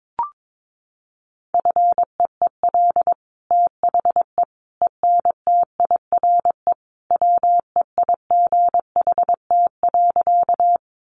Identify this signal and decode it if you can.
Morse code, heard as a keyed tone, about 22 wpm, 700 Hz. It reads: FEEL THE ENTIRE WEIGHT.
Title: Sentences from Top 800 Words 22wpm